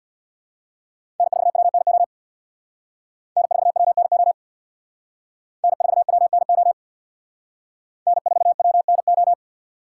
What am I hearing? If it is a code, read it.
Morse code, heard as a keyed tone, about 62 wpm, 700 Hz. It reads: N4WNO N4WNO N4WNO N4WNO
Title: Speed Racing - USA 1x3 Call Signs 50wpm